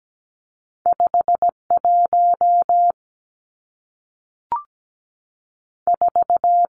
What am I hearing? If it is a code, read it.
Morse code, heard as a keyed tone, about 17 wpm, 700 Hz. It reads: 5 1 4
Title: Numbers - 2-Digit 17wpm